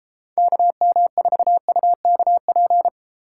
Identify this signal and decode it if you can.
Morse code, heard as a keyed tone, about 35 wpm, 700 Hz. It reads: KM4UKP
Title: Speed Racing - USA 2x3 Call Signs 28wpm